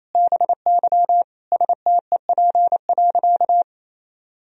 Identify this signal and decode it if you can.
Morse code, heard as a keyed tone, about 28 wpm, 700 Hz. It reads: BY STEP.
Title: Sentences from Top 1000 Words - Review Part 2 - 28wpm